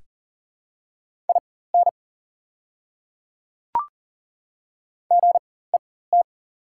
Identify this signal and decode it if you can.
Morse code, heard as a keyed tone, about 41 wpm, spaced as for 19 wpm, 700 Hz. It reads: IN GET
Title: 4x Character Spacing - Top 200 Words - 40wpm